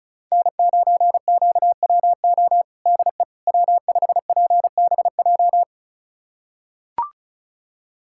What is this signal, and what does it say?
Morse code, heard as a keyed tone, about 34 wpm, 700 Hz. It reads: N9QWO DE W5PBJ
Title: USA Callsign DE Callsign - 5 Characters - 35wpm